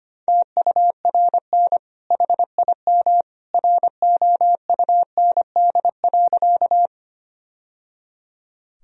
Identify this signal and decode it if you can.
Morse code, heard as a keyed tone, about 25 wpm, 700 Hz. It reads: TURN HIM ROUND.
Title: Sentences from Top 1000 Words - Review Part 1 - 25wpm